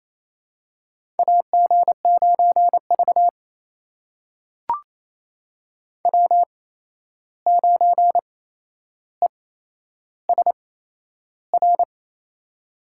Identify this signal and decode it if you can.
Morse code, heard as a keyed tone, about 28 wpm, spaced as for 11 wpm, 700 Hz. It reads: AG9V W9ESR
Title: USA Call Signs - 8x Character Spacing - 28wpm